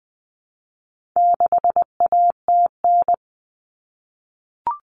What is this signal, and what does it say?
Morse code, heard as a keyed tone, about 20 wpm, 700 Hz. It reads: 6ATN
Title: All Cut Numbers - 4-Digit 20wpm